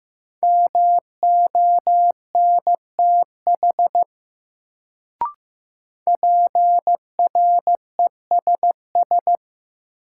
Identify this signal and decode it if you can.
Morse code, heard as a keyed tone, about 15 wpm, 700 Hz. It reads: MONTH PRESS
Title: Sentences from Top 800 Words - New Word Drill - 15wpm